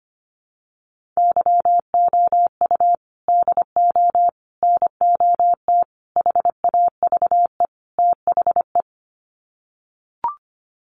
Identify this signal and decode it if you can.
Morse code, heard as a keyed tone, about 25 wpm, 700 Hz. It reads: YOU DO NOT HAVE THE
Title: Sets of 5 Words 25wpm